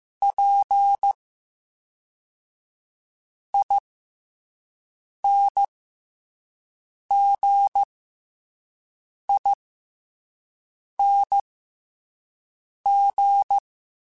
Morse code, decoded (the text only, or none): PINGING